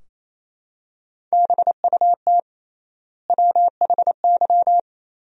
Morse code, heard as a keyed tone, about 27 wpm, 700 Hz. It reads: BUT WHY
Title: 3x Word-Spacing - Sentences from Top 200 Words - 28wpm